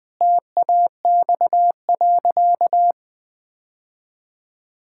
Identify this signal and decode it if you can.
Morse code, heard as a keyed tone, about 20 wpm, 700 Hz. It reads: TAX.
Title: Sentences from Top 1000 Words - Review Part 6 - 20wpm